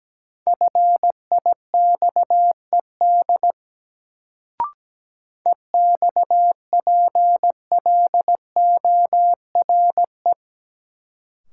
Morse code, heard as a keyed tone, about 17 wpm, 700 Hz. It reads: FIXED EXPLORE
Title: Sentences from Top 1100 Words - New Word Drill - 17wpm